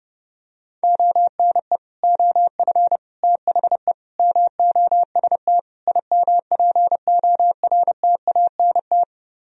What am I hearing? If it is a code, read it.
Morse code, heard as a keyed tone, about 29 wpm, 700 Hz. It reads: ONE OF THE MOST IMPORTANT